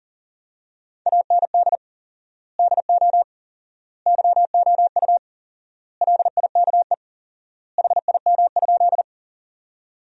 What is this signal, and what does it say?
Morse code, heard as a keyed tone, about 40 wpm, 700 Hz. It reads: AND DO YOU LIKE HIM?
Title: 4x Word-Spacing - Sentences from Top 100 Words - 40wpm